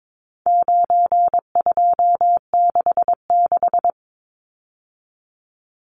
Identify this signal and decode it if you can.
Morse code, heard as a keyed tone, about 22 wpm, 700 Hz. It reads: 9266
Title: Numbers - 4-Digit 22wpm